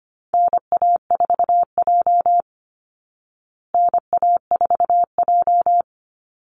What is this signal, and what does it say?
Morse code, heard as a keyed tone, about 25 wpm, 700 Hz. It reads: NA4J NA4J
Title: Call Signs Repeated 2x - 25wpm